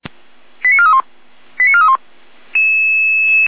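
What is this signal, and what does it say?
Five-tone selective call (ZVEI), heard as quick runs of five tones, tones of about 80 ms. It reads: 89431 89431
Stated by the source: ZVEI five-tone selective call received on an SDR.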